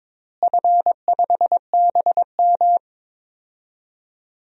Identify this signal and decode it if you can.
Morse code, heard as a keyed tone, about 22 wpm, 700 Hz. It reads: F5BM
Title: DXCC-100 Call Signs - 4 Characters - Repeated 3x - 22wpm